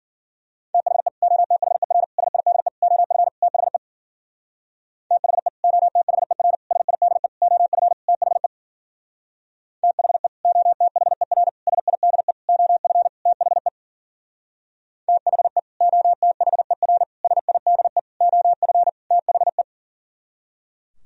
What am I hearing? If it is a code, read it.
Morse code, heard as a keyed tone, about 51 wpm, 700 Hz. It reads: THE OTHER SIDE OF THE THE OTHER SIDE OF THE THE OTHER SIDE OF THE THE OTHER SIDE OF THE